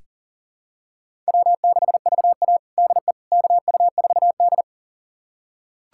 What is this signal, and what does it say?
Morse code, heard as a keyed tone, about 40 wpm, 700 Hz. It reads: W6VA DE KU4D
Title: USA Callsign DE Callsign - 4 Characters - 40wpm